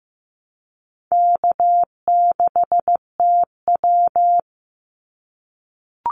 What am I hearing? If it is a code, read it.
Morse code, heard as a keyed tone, about 15 wpm, 700 Hz. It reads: K6TW
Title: NA SOTA Activator Callsigns 15wpm